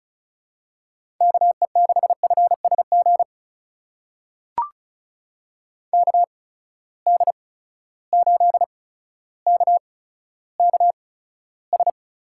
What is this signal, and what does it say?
Morse code, heard as a keyed tone, about 36 wpm, spaced as for 15 wpm, 700 Hz. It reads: KE6FSG KD8KKS
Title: USA Call Signs - 8x Character Spacing - 35wpm